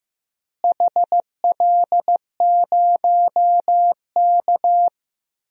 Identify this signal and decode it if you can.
Morse code, heard as a keyed tone, about 15 wpm, 700 Hz. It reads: HL0K
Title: DXCC-100 Call Signs - 4 Characters - Repeated 3x - 15wpm